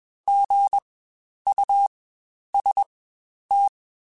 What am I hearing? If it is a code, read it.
Morse code, heard as a keyed tone, about 20 wpm, spaced as for 10 wpm, 800 Hz. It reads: GUST
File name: Spaced Alphas